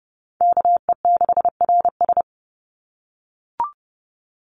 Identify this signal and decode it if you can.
Morse code, heard as a keyed tone, about 30 wpm, 700 Hz. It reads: KE6RS
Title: NA SOTA Chaser Callsigns 30wpm